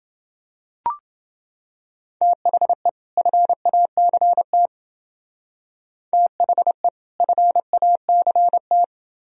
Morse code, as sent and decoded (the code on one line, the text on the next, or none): - .... / ..-. .- -.-. - / - .... / ..-. .- -.-. -
THE FACT THE FACT